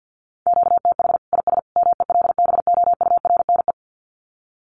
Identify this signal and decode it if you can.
Morse code, heard as a keyed tone, about 50 wpm, 700 Hz. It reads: QTH IS MELBOURNE
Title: Common QSO Phrases 50wpm